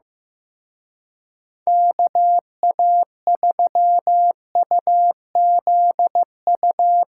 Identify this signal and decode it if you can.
Morse code, heard as a keyed tone, about 16 wpm, 700 Hz. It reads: KA3UZU
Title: Speed Racing - USA 2x3 Call Signs 15wpm